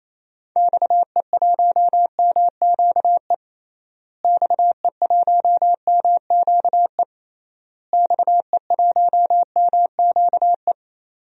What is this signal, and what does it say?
Morse code, heard as a keyed tone, about 28 wpm, 700 Hz. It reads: XE1MQE XE1MQE XE1MQE
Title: DXCC-100 Call Signs - 6 Characters - Repeated 3x - 28wpm